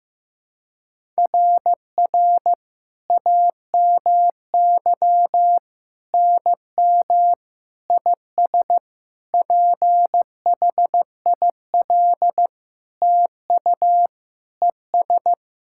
Morse code, heard as a keyed tone, about 15 wpm, 700 Hz. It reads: RR AMY NM IS PHIL TU ES
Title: SOTA QSO Practice 15wpm